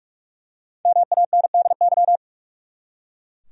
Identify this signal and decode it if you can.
Morse code, heard as a keyed tone, about 41 wpm, 700 Hz. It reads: MANDY
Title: USA Common Names - Max 5 Letters - 45wpm